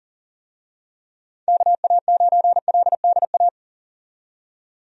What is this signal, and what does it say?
Morse code, heard as a keyed tone, about 38 wpm, 700 Hz. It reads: KA9LDA
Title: USA Call Signs - 6 Characters - 40wpm